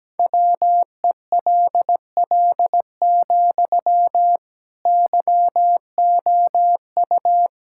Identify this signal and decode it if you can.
Morse code, heard as a keyed tone, about 17 wpm, 700 Hz. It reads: WELL, YOU
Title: Sentences from Top 1000 Words - Review Part 6 - 17wpm